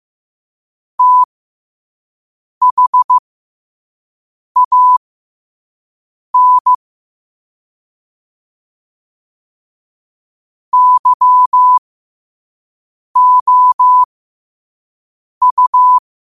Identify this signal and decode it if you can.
Morse code, heard as a keyed tone, about 15 wpm, spaced as for 5 wpm, 1000 Hz. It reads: THAN YOU
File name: qotdfiles / 5wpm